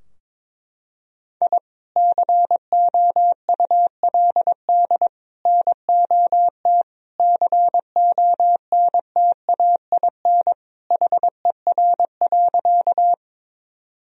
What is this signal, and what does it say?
Morse code, heard as a keyed tone, about 22 wpm, 700 Hz. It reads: I COULD NOT CONTAIN HER.
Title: Sentences from Top 1000 Words - Review Part 2 - 22wpm